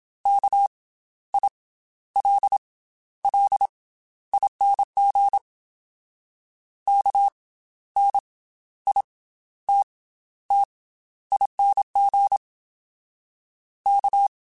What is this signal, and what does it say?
Morse code, heard as a keyed tone, about 27 wpm, spaced as for 13 wpm, 800 Hz. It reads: KILLING KNITTING K
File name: Special INGs / Special INGs-mp3s / Slow ing @25